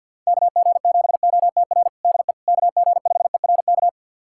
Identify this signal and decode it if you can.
Morse code, heard as a keyed tone, about 52 wpm, 700 Hz. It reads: KG7OTR DE KG5ERK
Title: USA Callsign DE Callsign - 6 Characters - 50wpm